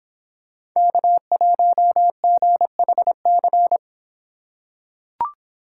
Morse code, heard as a keyed tone, about 24 wpm, 700 Hz. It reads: K1GHC